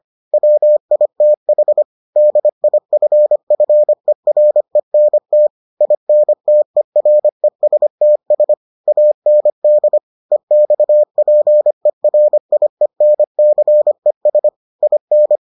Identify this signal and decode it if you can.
Morse code, heard as a keyed tone, about 25 wpm, 600 Hz. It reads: WITH DIFFERENT INTERESTS AND EXPERIENCES IN